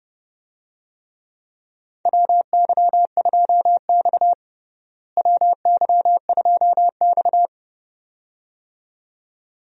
Morse code, heard as a keyed tone, about 30 wpm, 700 Hz. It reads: WY2X WY2X
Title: USA Call Signs - 4 Characters - Repeated 2x - 30wpm